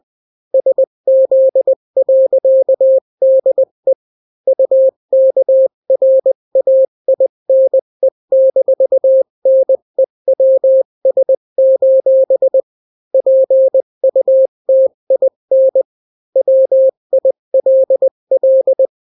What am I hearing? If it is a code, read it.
Morse code, heard as a keyed tone, about 20 wpm, 530 Hz. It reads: SZ.DE UKRAINE-NEWS: PUTIN WILL
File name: daily / 18wpm